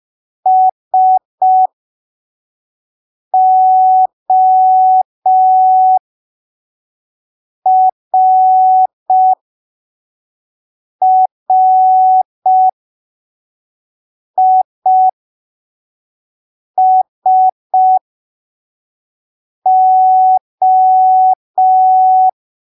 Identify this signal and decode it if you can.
Morse code, heard as a keyed tone, about 5 wpm, spaced as for 3 wpm, 750 Hz. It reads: SORRISO
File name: arquivos / texto / texto 15 05ppm